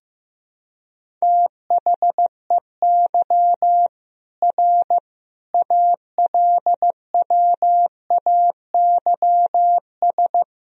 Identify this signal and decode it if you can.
Morse code, heard as a keyed tone, about 15 wpm, 700 Hz. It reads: THEY R ALWAYS
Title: Sentences from Top 300 Words and Abbreviations 15wpm